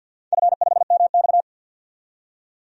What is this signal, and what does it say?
Morse code, heard as a keyed tone, about 48 wpm, 700 Hz. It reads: F5MX
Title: DXCC-100 Call Signs - 4 Characters - 50wpm